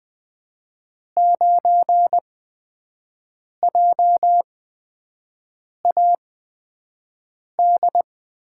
Morse code, heard as a keyed tone, about 20 wpm, spaced as for 5 wpm, 700 Hz. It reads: 9JAD
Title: USA Call Signs - 6 Characters - 8x Character Spacing - 20wpm